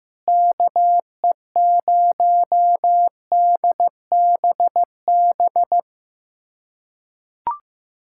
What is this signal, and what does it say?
Morse code, heard as a keyed tone, about 15 wpm, 700 Hz. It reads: KE0DBB
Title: USA Callsign DE Callsign - 6 Characters - 15wpm